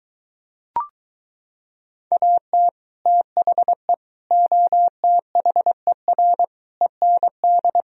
Morse code, heard as a keyed tone, about 22 wpm, 700 Hz. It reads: AT THE OTHER END